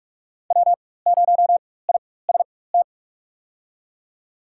Morse code, heard as a keyed tone, about 46 wpm, spaced as for 21 wpm, 700 Hz. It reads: W0IST
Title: USA Call Signs - 5 Characters - 4x Character Spacing - 45wpm